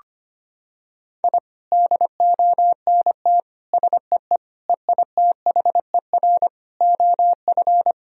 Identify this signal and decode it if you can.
Morse code, heard as a keyed tone, about 25 wpm, 700 Hz. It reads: I DONT SEE EITHER OF
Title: Sentences from Top 1000 Words - Review Part 4 - 25wpm